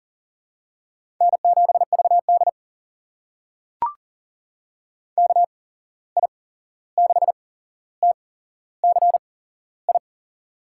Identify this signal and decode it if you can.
Morse code, heard as a keyed tone, about 41 wpm, spaced as for 15 wpm, 700 Hz. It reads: N7VD KI6TCI
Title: USA Call Signs - 8x Character Spacing - 40wpm